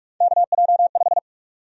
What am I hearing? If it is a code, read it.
Morse code, heard as a keyed tone, about 46 wpm, 700 Hz. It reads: KJ5
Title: USA Call Sign Prefixes - 45wpm